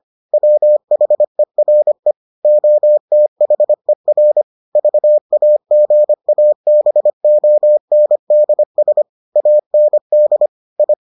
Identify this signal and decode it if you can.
Morse code, heard as a keyed tone, about 25 wpm, 600 Hz. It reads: WHERE OTHER VAGABONDS AND I